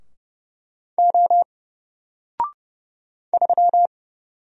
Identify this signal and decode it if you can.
Morse code, heard as a keyed tone, about 30 wpm, 700 Hz. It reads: O 3
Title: Single Letter-Number Alt 30wpm